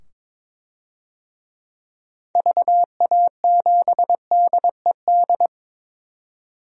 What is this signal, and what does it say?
Morse code, heard as a keyed tone, about 22 wpm, 700 Hz. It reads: VA7DED
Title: DXCC-100 Call Signs - 6 Characters - 22wpm